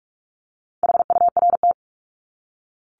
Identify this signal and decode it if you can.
Morse code, heard as a keyed tone, about 44 wpm, 700 Hz. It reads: HURT
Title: Top 1751-2000 Words 45wpm